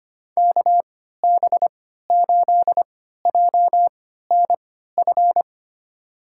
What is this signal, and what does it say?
Morse code, heard as a keyed tone, about 25 wpm, spaced as for 14 wpm, 700 Hz. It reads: KB8JNF
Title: USA Call Signs - 6 Characters - 3x Character Spacing - 25wpm